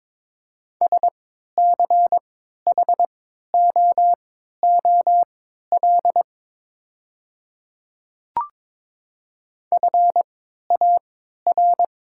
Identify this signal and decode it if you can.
Morse code, heard as a keyed tone, about 23 wpm, spaced as for 13 wpm, 690 Hz. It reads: SCHOOL FAR
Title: 3x Character Spacing - Top 300 Words - 22wpm